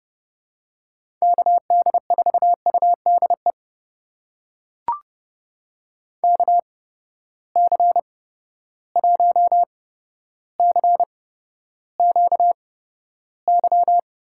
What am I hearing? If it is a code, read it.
Morse code, heard as a keyed tone, about 30 wpm, spaced as for 13 wpm, 700 Hz. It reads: KD4UDE KC1CQY
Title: USA Call Signs - 6 Characters - 8x Character Spacing - 30wpm